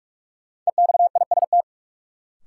Morse code, heard as a keyed tone, about 47 wpm, 700 Hz. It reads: EXIST
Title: Sentences from Top 1100 Words - New Word Drill Alt - 45wpm